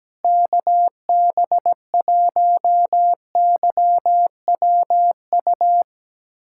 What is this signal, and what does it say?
Morse code, heard as a keyed tone, about 17 wpm, 700 Hz. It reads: KB1YWU
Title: USA Callsign DE Callsign - 6 Characters - 17wpm